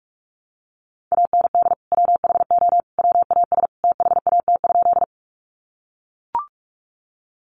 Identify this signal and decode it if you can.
Morse code, heard as a keyed tone, about 44 wpm, 700 Hz. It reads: AND WHO WAS THAT?
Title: Sentences from Top 100 Words 45wpm